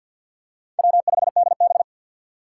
Morse code, heard as a keyed tone, about 53 wpm, 700 Hz. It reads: W5DB
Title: USA Call Signs - 4 Characters - 2x Character Spacing - 50wpm